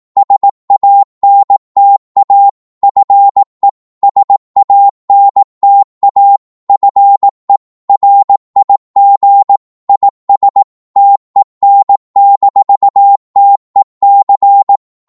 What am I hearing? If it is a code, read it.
Morse code, heard as a keyed tone, about 18 wpm, 800 Hz. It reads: SANTA FE SANTA FE RIG IS TEN-TEC
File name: Morse / qso-18wpm